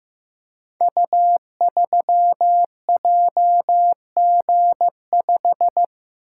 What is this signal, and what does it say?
Morse code, heard as a keyed tone, about 15 wpm, 700 Hz. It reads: U3JG5
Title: Code Groups - 5 Letter:Numbers - 15wpm